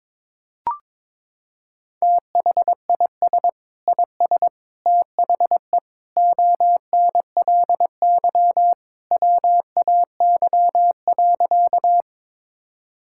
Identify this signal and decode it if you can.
Morse code, heard as a keyed tone, about 21 wpm, 700 Hz. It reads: THIS IS THE ONLY WAY.
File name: practice / 22wpm / Sentences with 2-Word Phrases - 22wpm